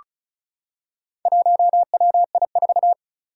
Morse code, heard as a keyed tone, about 36 wpm, 700 Hz. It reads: 1WI4
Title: Four Challenging Letter-Number Combinations 35wpm